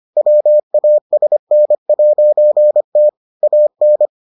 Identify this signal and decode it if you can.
Morse code, heard as a keyed tone, about 25 wpm, 600 Hz. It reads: WASN'T AN